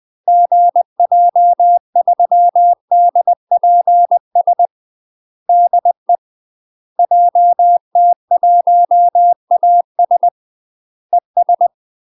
Morse code, heard as a keyed tone, about 20 wpm, 700 Hz. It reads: GJ3DPS DE JT1AS ES
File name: UK MASTER QSOs TRAINING / 20WPM / UK MASTER QSOs 0032